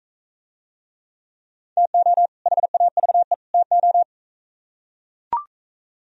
Morse code, heard as a keyed tone, about 40 wpm, 700 Hz. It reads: TO HAVE TO